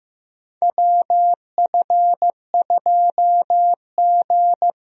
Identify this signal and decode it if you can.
Morse code, heard as a keyed tone, about 15 wpm, 700 Hz. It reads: WF2G
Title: USA Call Signs - 4 Characters - 15wpm